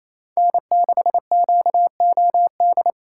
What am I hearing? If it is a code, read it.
Morse code, heard as a keyed tone, about 28 wpm, 700 Hz. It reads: N6QOD
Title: USA Call Signs - 5 Characters - 2x Character Spacing - 28wpm